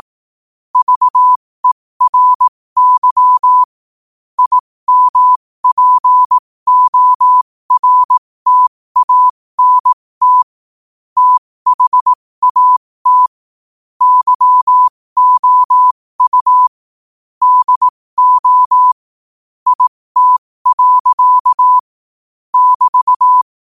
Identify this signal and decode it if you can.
Morse code, heard as a keyed tone, about 18 wpm, spaced as for 15 wpm, 1000 Hz. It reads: VERY IMPORTANT THAT YOU DO IT. <BT>
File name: qotdfiles / 15wpm